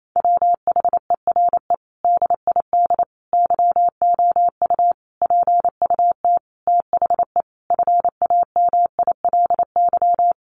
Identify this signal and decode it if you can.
Morse code, heard as a keyed tone, about 28 wpm, 700 Hz. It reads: WHERE DID YOU PUT THE FAMILY
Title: Sentences from Top 1200 Words 28wpm